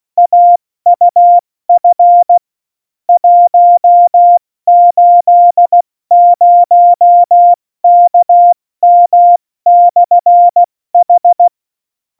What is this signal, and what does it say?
Morse code, heard as a keyed tone, about 16 wpm, 700 Hz. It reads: AUF 180KM/H